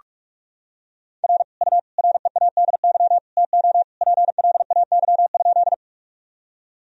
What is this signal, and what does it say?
Morse code, heard as a keyed tone, about 46 wpm, 700 Hz. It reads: R U READY TO PLAY?
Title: Sentences from Top 400 Words and Abbreviations 45wpm